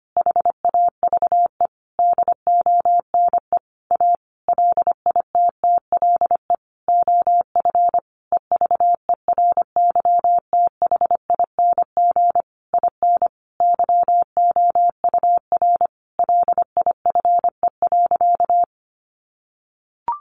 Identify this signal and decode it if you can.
Morse code, heard as a keyed tone, about 25 wpm, 700 Hz. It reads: HAVE DONE A LITTLE OF EVERYTHING IN YOUR LIFE.
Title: Sentences from Top 700 Words 25wpm